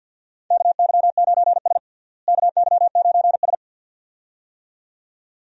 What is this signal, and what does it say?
Morse code, heard as a keyed tone, about 50 wpm, 700 Hz. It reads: KY9S KY9S
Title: USA Call Signs - 4 Characters - Repeated 2x - 50wpm